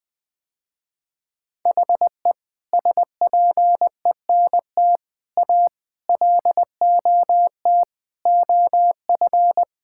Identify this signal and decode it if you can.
Morse code, heard as a keyed tone, about 20 wpm, 700 Hz. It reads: HE SPENT A LOT OF
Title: Sentences with 4-Word Phrases - Spoken Beginning - 20wpm